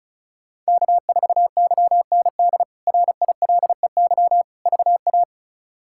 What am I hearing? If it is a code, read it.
Morse code, heard as a keyed tone, about 36 wpm, 700 Hz. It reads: K4YND RILEY VA
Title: K1USN SST QSO Practice - 35wpm